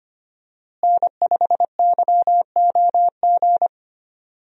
Morse code, heard as a keyed tone, about 25 wpm, 700 Hz. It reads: N5YOG
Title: USA Call Signs - 5 Characters - Repeated 2x - 25wpm